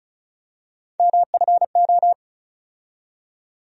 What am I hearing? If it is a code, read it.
Morse code, heard as a keyed tone, about 35 wpm, 700 Hz. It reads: MFO
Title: USA Call Sign Postfixes - 3 Letters 35wpm